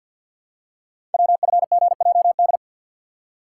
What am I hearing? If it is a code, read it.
Morse code, heard as a keyed tone, about 48 wpm, 700 Hz. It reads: WFGJD
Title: Code Groups - 5 Letter:Numbers - 50wpm